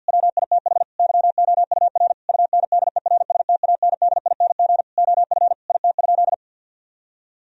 Morse code, heard as a keyed tone, about 50 wpm, 700 Hz. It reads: WITH YOUR UNDERSTANDING OF IT?
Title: Sentences from Top 1000 Words - Review Part 3 - 50wpm